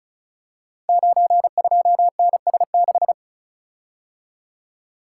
Morse code, heard as a keyed tone, about 34 wpm, 700 Hz. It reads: 92NS6
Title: Code Groups - 5 Letter:Numbers - 35wpm